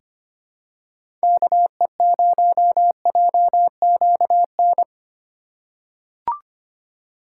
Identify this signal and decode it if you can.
Morse code, heard as a keyed tone, about 25 wpm, 700 Hz. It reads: KE0JQN
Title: USA Call Signs - 6 Characters - 3x Character Spacing - 25wpm